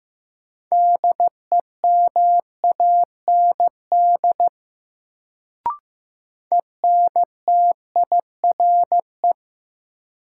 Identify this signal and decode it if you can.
Morse code, heard as a keyed tone, about 15 wpm, 700 Hz. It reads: DEMAND ENTIRE
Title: Sentences from Top 800 Words - New Word Drill - 15wpm